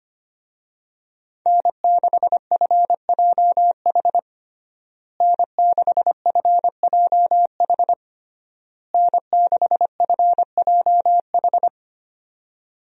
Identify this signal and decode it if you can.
Morse code, heard as a keyed tone, about 25 wpm, 700 Hz. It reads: N6FJH N6FJH N6FJH
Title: USA Call Signs - 5 Characters - Repeated 3x - 25wpm